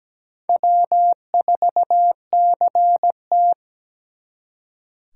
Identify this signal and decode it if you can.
Morse code, heard as a keyed tone, about 20 wpm, 700 Hz. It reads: W4CT
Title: Speed Racing - USA 1x2 Call Signs 17wpm